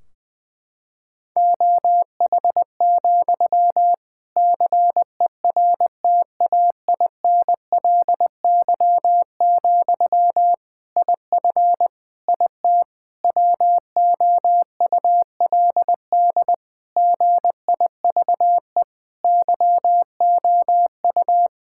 Morse code, heard as a keyed tone, about 20 wpm, 700 Hz. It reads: OH, CERTAINLY, IF IT WOULD GIVE YOU
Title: Sentences from Top 1600 Words 20wpm